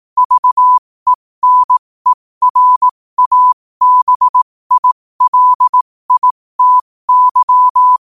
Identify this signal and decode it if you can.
Morse code, heard as a keyed tone, about 18 wpm, spaced as for 15 wpm, 1000 Hz. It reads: VENERABILITY
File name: qotdfiles / 15wpm